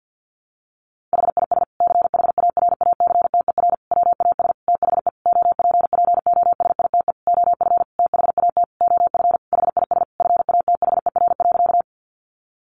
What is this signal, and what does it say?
Morse code, heard as a keyed tone, about 50 wpm, 700 Hz. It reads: HIS CHARACTER WAS THE OPPOSITE OF THAT OF HIS FATHER.
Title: Sentences from Top 1300 Words 50wpm